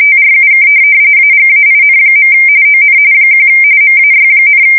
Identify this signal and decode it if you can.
RTTY, about 45.45 baud. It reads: TRUK AGAIN. ON 1 MAY,
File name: fsk170 / USS-Massachusetts